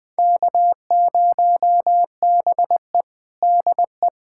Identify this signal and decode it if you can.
Morse code, heard as a keyed tone, about 20 wpm, 700 Hz. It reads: K0BE DE
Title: USA Callsign DE Callsign - 4 Characters - 20wpm